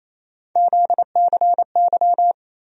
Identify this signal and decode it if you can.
Morse code, heard as a keyed tone, about 28 wpm, 700 Hz. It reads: ZCY
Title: USA Call Sign Postfixes - 3 Letters 28wpm